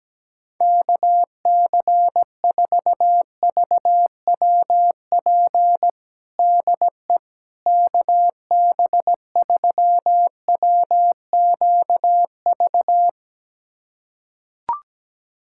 Morse code, heard as a keyed tone, about 17 wpm, 700 Hz. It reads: KC4VWP DE KB3WQV
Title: USA Callsign DE Callsign 17wpm